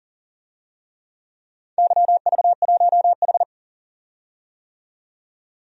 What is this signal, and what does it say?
Morse code, heard as a keyed tone, about 42 wpm, 700 Hz. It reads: YV1H
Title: DXCC-100 Call Signs - 4 Characters - 40wpm